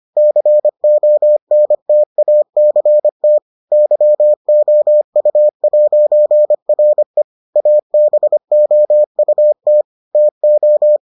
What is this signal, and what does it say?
Morse code, heard as a keyed tone, about 25 wpm, 600 Hz. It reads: CONTACT YOU'RE ABOUT TO